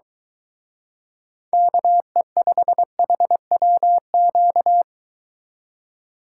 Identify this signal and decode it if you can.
Morse code, heard as a keyed tone, about 23 wpm, 700 Hz. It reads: KE5HWQ